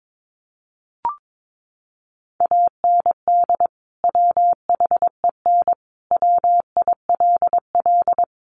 Morse code, heard as a keyed tone, about 22 wpm, 700 Hz. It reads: AND WHEN WILL
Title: Sentences from Top 100 Words 22wpm